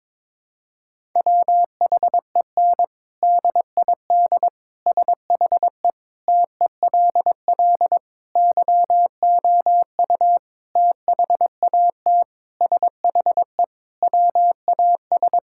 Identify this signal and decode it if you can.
Morse code, heard as a keyed tone, about 22 wpm, 700 Hz. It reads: WHEN DID SHE TELL YOU THAT SHE WAS
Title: Sentences from Top 1000 Words - Review Part 6 - 22wpm